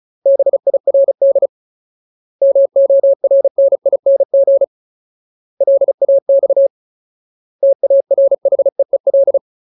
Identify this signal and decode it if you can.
Morse code, heard as a keyed tone, about 37 wpm, 550 Hz. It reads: BIRD MORNING LAX TARHEEL
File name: Ultimate Word List / 35WPM Ultimate